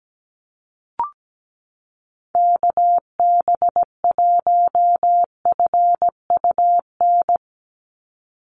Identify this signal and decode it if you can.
Morse code, heard as a keyed tone, about 17 wpm, 700 Hz. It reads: KB1FUN
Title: USA Call Signs 17wpm